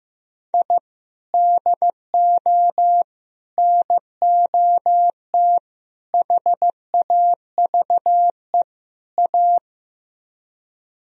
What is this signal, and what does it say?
Morse code, heard as a keyed tone, about 15 wpm, 700 Hz. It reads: I DO NOT HAVE A
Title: Speed-Racing - Sets of 5 Words 15wpm